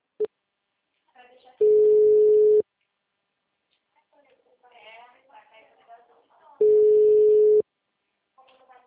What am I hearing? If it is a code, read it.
Ringback tone (Europe, 425 Hz), with speech.
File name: out- -2023- - - .56143